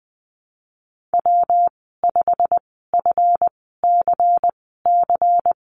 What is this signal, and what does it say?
Morse code, heard as a keyed tone, about 20 wpm, spaced as for 15 wpm, 700 Hz. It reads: W5FCC
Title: 173 Lesson 28g - Number 9 - Call Signs 20wpm